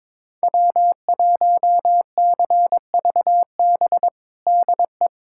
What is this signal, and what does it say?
Morse code, heard as a keyed tone, about 22 wpm, 700 Hz. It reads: W1CVB DE
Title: USA Callsign DE Callsign - 5 Characters - 22wpm